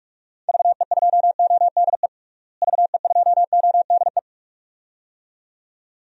Morse code, heard as a keyed tone, about 46 wpm, 700 Hz. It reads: VE2ODE VE2ODE
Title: DXCC-100 Call Signs - 6 Characters - Repeated 2x - 45wpm